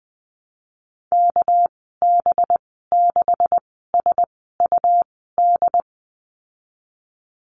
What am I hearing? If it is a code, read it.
Morse code, heard as a keyed tone, about 20 wpm, spaced as for 14 wpm, 700 Hz. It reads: KB6SUD
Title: 219 Lesson 35g - Number 0 - Call Signs 20wpm